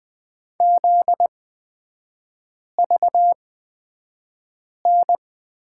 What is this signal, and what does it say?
Morse code, heard as a keyed tone, about 20 wpm, spaced as for 5 wpm, 700 Hz. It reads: ZVN
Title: 243 Lesson 39e - Letter z - Three Characters 20wpm